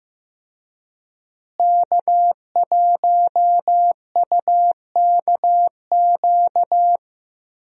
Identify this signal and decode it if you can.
Morse code, heard as a keyed tone, about 15 wpm, 700 Hz. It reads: K1UKQ
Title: USA Call Signs - 5 Characters - Repeated 2x - 15wpm